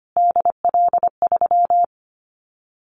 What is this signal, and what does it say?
Morse code, heard as a keyed tone, about 25 wpm, 700 Hz. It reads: DL3
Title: Callsign Prefixes for Top 25 DXCC - Repeated 3x - 25wpm